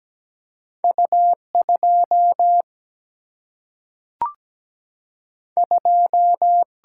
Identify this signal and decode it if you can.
Morse code, heard as a keyed tone, about 17 wpm, 700 Hz. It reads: U2 2
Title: Single-Letter Single-Number 17wpm